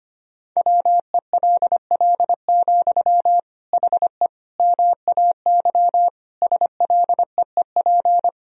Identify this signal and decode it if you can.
Morse code, heard as a keyed tone, about 25 wpm, 700 Hz. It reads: WELL, HE MAY SLEEP
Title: Sentences from Top 1000 Words - Review Part 4 - 25wpm